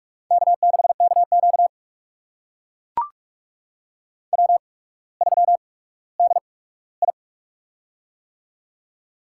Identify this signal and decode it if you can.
Morse code, heard as a keyed tone, about 42 wpm, spaced as for 19 wpm, 700 Hz. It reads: K6KQ W3DI
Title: USA Call Signs - 8x Character Spacing - 45wpm